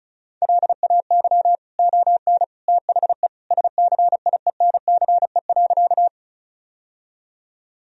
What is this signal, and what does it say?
Morse code, heard as a keyed tone, about 36 wpm, 700 Hz. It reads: LAY ON THE SCIENCE.
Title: Sentences from Top 1000 Words - Review Part 2 - 35wpm